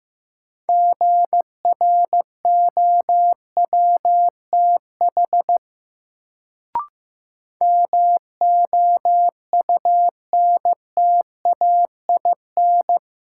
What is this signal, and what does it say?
Morse code, heard as a keyed tone, about 15 wpm, 700 Hz. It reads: GROWTH MOUNTAIN
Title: Sentences from Top 1300 Words - New Word Drill Alt - 15wpm